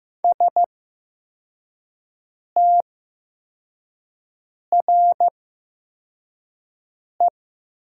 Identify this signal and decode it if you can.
Morse code, heard as a keyed tone, about 15 wpm, spaced as for 4 wpm, 700 Hz. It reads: STRE
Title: 8x Character Spacing - Top 500 Words - 15wpm